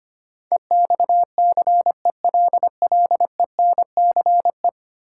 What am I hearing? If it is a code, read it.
Morse code, heard as a keyed tone, about 25 wpm, 700 Hz. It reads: EXCELLENCE
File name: 10 Letter Words - First 3 Letters Given - 25wpm